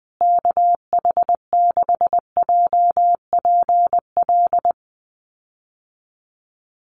Morse code, heard as a keyed tone, about 20 wpm, 700 Hz. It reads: KH6JPL
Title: USA Call Signs Repeated 3x 20wpm